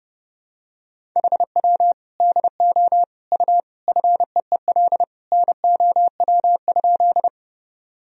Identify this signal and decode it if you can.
Morse code, heard as a keyed tone, about 32 wpm, 700 Hz. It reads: HW DO U FEEL NOW?
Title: Sentences from Top 400 Words and Abbreviations 30wpm